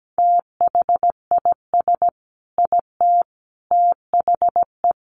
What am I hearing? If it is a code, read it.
Morse code, heard as a keyed tone, about 17 wpm, 700 Hz. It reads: THIS IT THE
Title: Sentences from Top 800 Words 17wpm